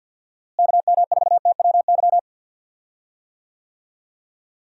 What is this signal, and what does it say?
Morse code, heard as a keyed tone, about 52 wpm, 700 Hz. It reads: KM4TWY
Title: USA Call Signs - 6 Characters - Repeated 2x - 50wpm